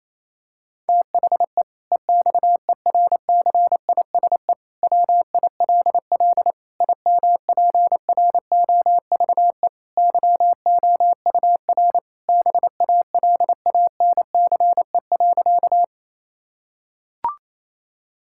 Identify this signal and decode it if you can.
Morse code, heard as a keyed tone, about 27 wpm, 700 Hz. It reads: THE EXERCISE WILL IMPROVE YOUR BALANCE.
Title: Sentences from Top 1000 Words - Review Part 6 - 28wpm